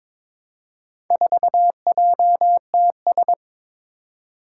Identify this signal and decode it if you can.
Morse code, heard as a keyed tone, about 22 wpm, 700 Hz. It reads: 4JTS
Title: Four Challenging Letter-Number Combinations 22wpm